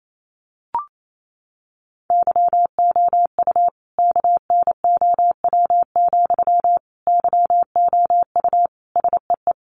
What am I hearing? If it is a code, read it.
Morse code, heard as a keyed tone, about 28 wpm, 700 Hz. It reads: YOU KNOW, YOU SEE
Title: Sentences from Top 200 Words 28wpm